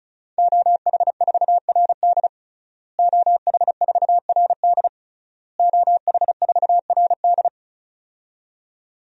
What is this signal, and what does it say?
Morse code, heard as a keyed tone, about 36 wpm, 700 Hz. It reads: OH4RD OH4RD OH4RD
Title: DXCC-100 Call Signs - 5 Characters - Repeated 3x - 35wpm